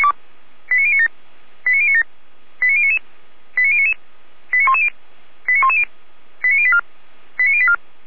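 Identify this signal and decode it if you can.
Five-tone selective call (ZVEI), heard as quick runs of five tones, tones of about 70 ms. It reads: 89097 89097 89099 89099 89110 89110 89084 89084
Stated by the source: ZVEI five-tone selective call 89080 received on an SDR.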